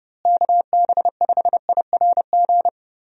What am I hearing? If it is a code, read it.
Morse code, heard as a keyed tone, about 32 wpm, 700 Hz. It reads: KB5IRG
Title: USA Call Signs - 6 Characters - Repeated 3x - 30wpm